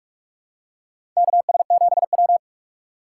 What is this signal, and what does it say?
Morse code, heard as a keyed tone, about 46 wpm, 700 Hz. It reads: KS7W
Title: USA Call Signs - 3x Character Spacing - 45wpm